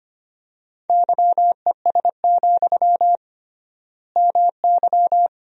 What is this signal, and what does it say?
Morse code, heard as a keyed tone, about 25 wpm, 700 Hz. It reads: YES, MY
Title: 3x Word-Spacing - Sentences from Top 500 Words - 25wpm